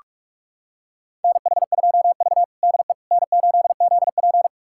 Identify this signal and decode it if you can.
Morse code, heard as a keyed tone, about 46 wpm, 700 Hz. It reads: NH2V DE N8ZP
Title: USA Callsign DE Callsign - 4 Characters - 45wpm